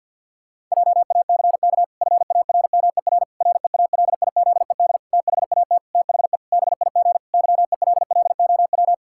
Morse code, heard as a keyed tone, about 50 wpm, 700 Hz. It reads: JACK FARMER REALIZED THAT THE BIG YELLOW
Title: Pangram Challenge v2 - 50wpm